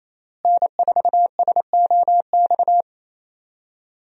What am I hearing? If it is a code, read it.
Morse code, heard as a keyed tone, about 28 wpm, 700 Hz. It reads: N4SOX
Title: USA Call Signs - 5 Characters - Repeated 3x - 28wpm